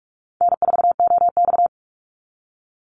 Morse code, heard as a keyed tone, about 46 wpm, 700 Hz. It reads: N4OX
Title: Easy Call Signs - 45wpm